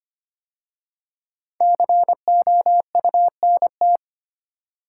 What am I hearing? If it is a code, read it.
Morse code, heard as a keyed tone, about 25 wpm, 700 Hz. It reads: COUNT
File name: 5 Letter Words - First 3 Letters Given - 25wpm